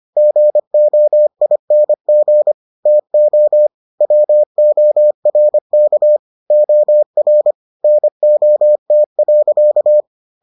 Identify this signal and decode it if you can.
Morse code, heard as a keyed tone, about 25 wpm, 600 Hz. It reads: GOING TO WORK OR NOT.